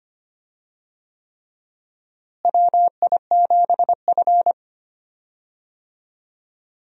Morse code, heard as a keyed tone, about 25 wpm, 700 Hz. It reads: WI7F
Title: USA Call Signs - 4 Characters - Repeated 3x - 25wpm